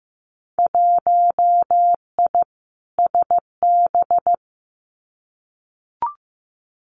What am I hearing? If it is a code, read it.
Morse code, heard as a keyed tone, about 15 wpm, 700 Hz. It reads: 1I SB
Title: Winter Field Day Exchanges 15wpm